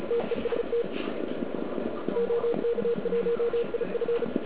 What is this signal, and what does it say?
Morse code, heard as a keyed tone, about 34 wpm, 470 Hz. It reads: TEST OK1IF
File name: ARCHIV / KX3 / KV PA